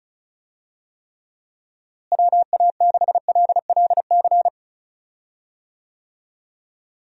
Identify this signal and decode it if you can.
Morse code, heard as a keyed tone, about 36 wpm, 700 Hz. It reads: WA6LLC
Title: USA Call Signs - 6 Characters - Repeated 3x - 35wpm